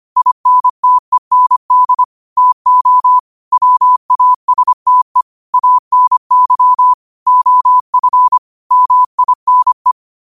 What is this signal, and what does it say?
Morse code, heard as a keyed tone, about 25 wpm, 1000 Hz. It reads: INTEND TO WASTE ANY OF MINE
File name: qotdfiles / 25wpm